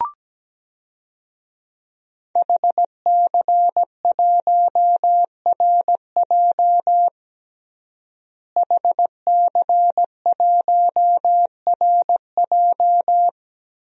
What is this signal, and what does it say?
Morse code, heard as a keyed tone, about 17 wpm, 700 Hz. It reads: HC1RJ HC1RJ
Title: DXCC-100 Call Signs - 5 Characters - Repeated 3x - 17wpm